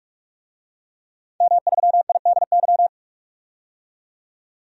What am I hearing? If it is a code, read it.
Morse code, heard as a keyed tone, about 46 wpm, 700 Hz. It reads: M3IDY
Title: DXCC-100 Call Signs - 5 Characters - 45wpm